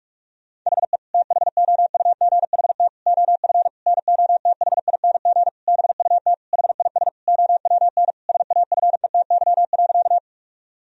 Morse code, heard as a keyed tone, about 46 wpm, 700 Hz. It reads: HE THOUGHT OF NOTHING BUT HIS OWN SAFETY.
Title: Sentences from Top 1000 Words - Review Part 4 - 45wpm